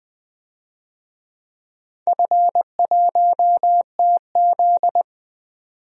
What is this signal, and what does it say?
Morse code, heard as a keyed tone, about 21 wpm, 700 Hz. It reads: F1TZ
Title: DXCC-100 Call Signs - 4 Characters - 20wpm